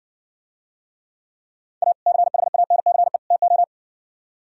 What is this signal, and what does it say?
Morse code, heard as a keyed tone, about 59 wpm, 700 Hz. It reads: A CHANCE TO